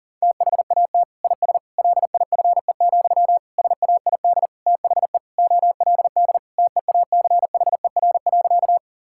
Morse code, heard as a keyed tone, about 40 wpm, 700 Hz. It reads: THAT IS LIFE, SAID THE OLD TEACHER.
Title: Sentences from Top 1000 Words - Review Part 6 - 40wpm